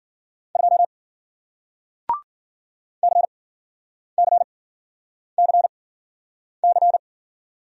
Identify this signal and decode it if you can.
Morse code, heard as a keyed tone, about 54 wpm, spaced as for 9 wpm, 700 Hz. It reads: 2 CCCC